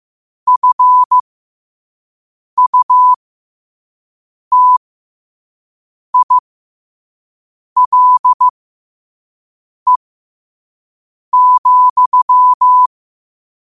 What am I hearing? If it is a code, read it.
Morse code, heard as a keyed tone, about 15 wpm, spaced as for 5 wpm, 1000 Hz. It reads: FUTILE,
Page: New quotes every day in morse code at 5 Words per minute.